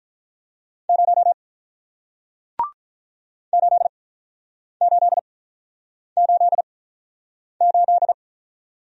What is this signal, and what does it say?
Morse code, heard as a keyed tone, about 48 wpm, spaced as for 9 wpm, 700 Hz. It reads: 0 8888